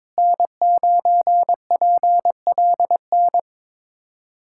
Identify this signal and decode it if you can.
Morse code, heard as a keyed tone, about 22 wpm, 700 Hz. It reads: N9PLN
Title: USA Callsign DE Callsign - 5 Characters - 22wpm